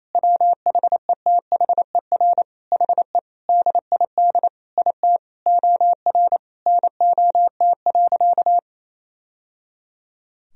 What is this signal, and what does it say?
Morse code, heard as a keyed tone, about 28 wpm, 700 Hz. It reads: WHETHER HE DID IT OR NOT.
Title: Sentences from Top 1000 Words - Review Part 3 - 28wpm